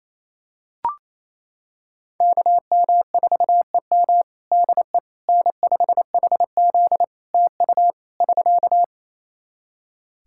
Morse code, heard as a keyed tone, about 27 wpm, 700 Hz. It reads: KM4EM DE N5HZ TU <SK>
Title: Practice QSO - Exchange 4 - Ending - 28wpm